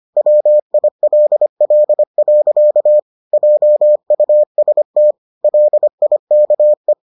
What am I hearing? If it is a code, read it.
Morse code, heard as a keyed tone, about 25 wpm, 600 Hz. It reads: WILL. JUST LIKE